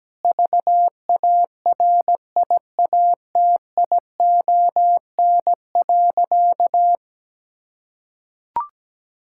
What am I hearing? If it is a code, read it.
Morse code, heard as a keyed tone, about 17 wpm, 700 Hz. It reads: VARIATION.
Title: Sentences from Top 1600 Words 17wpm